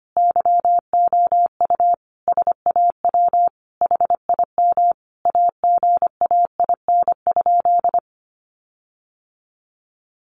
Morse code, heard as a keyed tone, about 25 wpm, 700 Hz. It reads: YOU SAW HIM AGAIN?
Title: Sentences from Top 1400 Words 25wpm